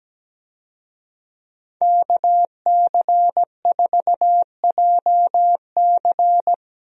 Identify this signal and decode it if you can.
Morse code, heard as a keyed tone, about 17 wpm, 700 Hz. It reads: KC4JC
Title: USA Call Signs - 5 Characters - 17wpm